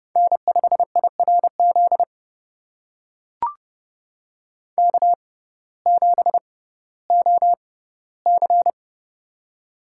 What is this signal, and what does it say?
Morse code, heard as a keyed tone, about 30 wpm, spaced as for 17 wpm, 700 Hz. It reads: N5IRZ K7OC
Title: USA Call Signs - 6x Character Spacing - 30wpm